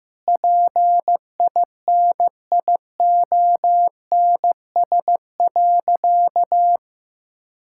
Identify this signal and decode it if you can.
Morse code, heard as a keyed tone, about 15 wpm, 700 Hz. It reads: PINIONS.
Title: Pangram Challenge v2 - 15wpm